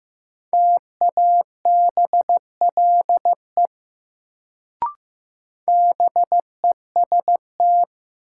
Morse code, heard as a keyed tone, about 15 wpm, 700 Hz. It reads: TABLE BEST
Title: Sentences from Top 400 Words - New Word Drill - 15wpm